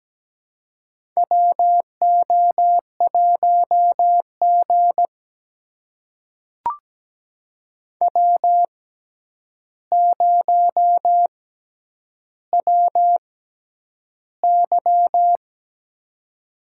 Morse code, heard as a keyed tone, about 17 wpm, spaced as for 9 wpm, 700 Hz. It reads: WO1G W0WY
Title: USA Call Signs - 4 Characters - 6x Character Spacing - 17wpm